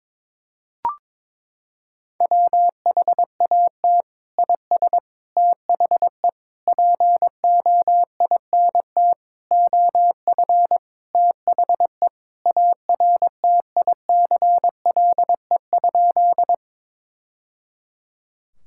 Morse code, heard as a keyed tone, about 22 wpm, 700 Hz. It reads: WHAT IS THE POINT OF THE ARTICLE?
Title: Sentences from Top 1000 Words - Review Part 5 - 22wpm